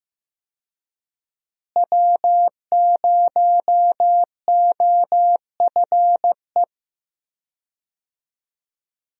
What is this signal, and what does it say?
Morse code, heard as a keyed tone, about 15 wpm, 700 Hz. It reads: W0OFE
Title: USA Call Signs - 5 Characters - Repeated 3x - 15wpm